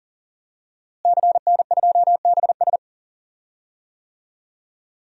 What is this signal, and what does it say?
Morse code, heard as a keyed tone, about 40 wpm, 700 Hz. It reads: CN2BS
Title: DXCC-100 Call Signs - 5 Characters - Repeated 2x - 40wpm